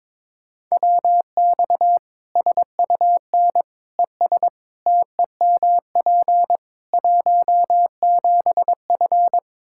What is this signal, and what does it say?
Morse code, heard as a keyed tone, about 22 wpm, 700 Hz. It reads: WX SUN ES TEMP 17F
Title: Practice QSO - Exchange 2 - Info - 22wpm